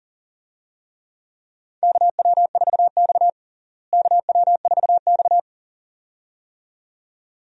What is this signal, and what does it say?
Morse code, heard as a keyed tone, about 40 wpm, 700 Hz. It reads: KW4X KW4X
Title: USA Call Signs - 4 Characters - Repeated 2x - 40wpm